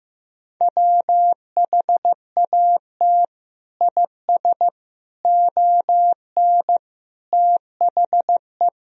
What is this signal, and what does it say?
Morse code, heard as a keyed tone, about 16 wpm, 700 Hz. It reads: WHAT IS ON THE
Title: Sentences from Top 1000 Words - Review Part 2 - 15wpm